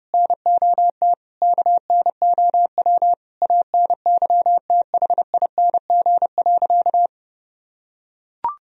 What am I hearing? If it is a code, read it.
Morse code, heard as a keyed tone, about 30 wpm, 700 Hz. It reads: NOT KNOW ANYTHING.
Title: Sentences from Top 1000 Words - Review Part 2 - 30wpm